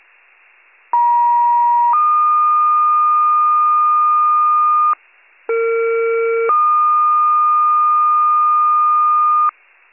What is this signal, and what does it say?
Two-tone page, heard as a one-second tone, then a three-second tone, about 950 then 1220 Hz, about 460 then 1220 Hz. It is heard from 0.9 to 9.5 s.